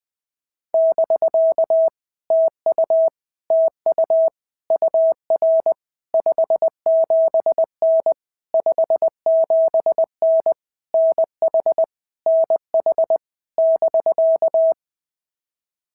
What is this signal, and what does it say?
Morse code, heard as a keyed tone, about 20 wpm, 650 Hz. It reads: <BK> TU TU UR 57N 57N NH NH <BK>
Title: POTA QSO Practice - 20wpm